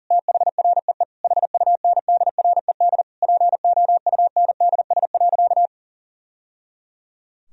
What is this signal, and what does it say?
Morse code, heard as a keyed tone, about 40 wpm, 700 Hz. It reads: THREE HUNDRED POUNDS.
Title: Sentences from Top 1000 Words - Review Part 4 - 40wpm